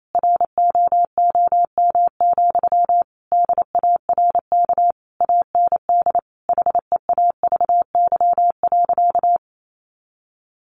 Morse code, heard as a keyed tone, about 28 wpm, 700 Hz. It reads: ROOM, DARK AND HEAVY.
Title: Sentences from Top 1400 Words 28wpm